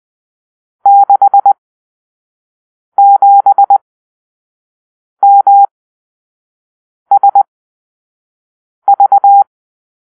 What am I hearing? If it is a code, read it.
Morse code, heard as a keyed tone, about 20 wpm, spaced as for 5 wpm, 800 Hz. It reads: 67MSV
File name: arrl / Grp-7 Drill-3